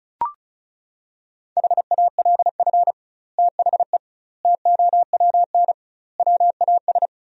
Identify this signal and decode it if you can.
Morse code, heard as a keyed tone, about 34 wpm, 700 Hz. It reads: HALF THE TOWN WAS
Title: 2x Word-Spacing - Sentences from Top 500 Words - 35wpm